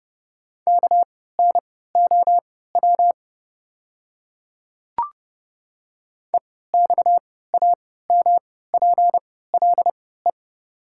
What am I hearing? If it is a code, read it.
Morse code, heard as a keyed tone, about 30 wpm, spaced as for 17 wpm, 690 Hz. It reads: KNOW EXAMPLE
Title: 3x Character Spacing - Top 500 Words - 30wpm